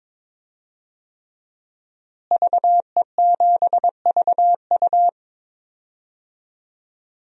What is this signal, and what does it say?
Morse code, heard as a keyed tone, about 22 wpm, 700 Hz. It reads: VE7VU
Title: DXCC-100 Call Signs - 5 Characters - Repeated 3x - 22wpm